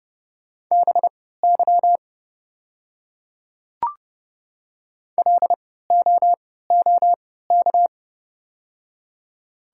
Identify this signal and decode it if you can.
Morse code, heard as a keyed tone, about 30 wpm, spaced as for 17 wpm, 700 Hz. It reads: BY LOOK
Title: 3x Character Spacing - Top 100 Words - 30wpm